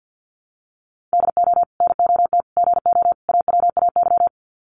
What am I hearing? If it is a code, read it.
Morse code, heard as a keyed tone, about 50 wpm, 700 Hz. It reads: DO NOT GO AWAY
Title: Sets of 4 Words 50wpm